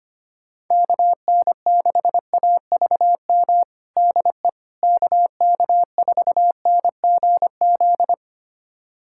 Morse code, heard as a keyed tone, about 25 wpm, 700 Hz. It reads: KN6AVM DE KK4NGZ
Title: USA Callsign DE Callsign 25wpm